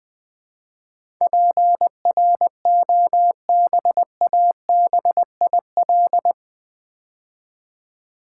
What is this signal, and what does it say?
Morse code, heard as a keyed tone, about 20 wpm, 700 Hz. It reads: PROBABIL
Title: Build A Word - 4 Letters At A Time - 11-Letter Words - 20wpm